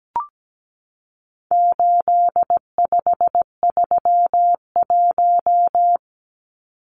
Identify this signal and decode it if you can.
Morse code, heard as a keyed tone, about 17 wpm, 700 Hz. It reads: 8531
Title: Numbers - 4-Digit 17wpm